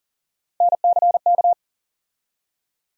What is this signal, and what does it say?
Morse code, heard as a keyed tone, about 40 wpm, 700 Hz. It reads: NCK
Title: USA Call Sign Postfixes - 3 Letters 40wpm